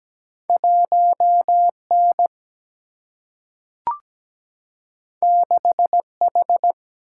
Morse code, heard as a keyed tone, about 17 wpm, 700 Hz. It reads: 1 N 6 H
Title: Single-Letter Single-Number 17wpm